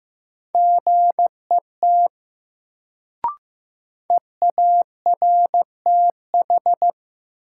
Morse code, heard as a keyed tone, about 15 wpm, 700 Hz. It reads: GET EARTH
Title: Sentences from Top 200 Words - New Word Drill - 15wpm